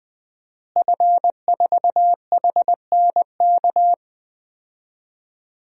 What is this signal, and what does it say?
Morse code, heard as a keyed tone, about 21 wpm, 700 Hz. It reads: F4HNK
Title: DXCC-100 Call Signs - 5 Characters - Repeated 2x - 20wpm